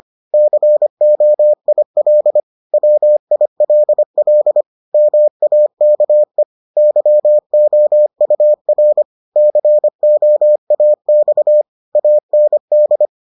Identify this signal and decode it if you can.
Morse code, heard as a keyed tone, about 25 wpm, 600 Hz. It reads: COIL WILL MAKE YOUR COAX AND